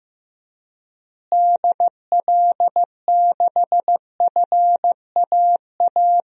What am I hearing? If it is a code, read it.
Morse code, heard as a keyed tone, about 14 wpm, 700 Hz. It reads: DL6FAA
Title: DXCC-100 Call Signs - 6 Characters - Repeated 3x - 15wpm